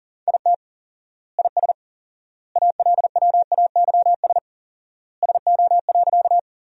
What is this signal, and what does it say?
Morse code, heard as a keyed tone, about 37 wpm, 700 Hz. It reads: IT IS ALWAYS SO.
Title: 4x Word-Spacing - Sentences from Top 300 Words - 40wpm